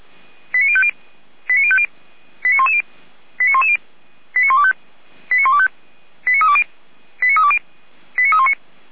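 Five-tone selective call (ZVEI), heard as quick runs of five tones, tones of about 70 ms. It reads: 89050 89050 89110 89110 89125 89125 89320 89320 89319
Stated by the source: ZVEI five-tone selective call received on SDR, 455 channel path.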